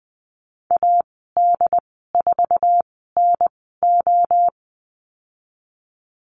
Morse code, heard as a keyed tone, about 20 wpm, spaced as for 15 wpm, 700 Hz. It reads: AD4NO
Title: 179 Lesson 29g - Slash - Call Signs 20wpm